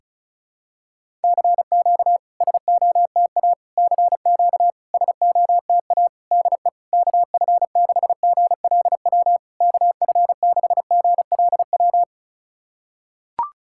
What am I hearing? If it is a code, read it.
Morse code, heard as a keyed tone, about 36 wpm, 690 Hz. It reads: CQ SOTA CQ SOTA DE KF6GLW KF6GLW
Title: SOTA QSO Practice 35wpm